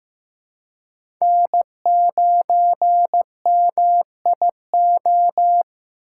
Morse code, heard as a keyed tone, about 18 wpm, 700 Hz. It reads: N9MIO
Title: Speed Racing - USA 1x3 Call Signs 15wpm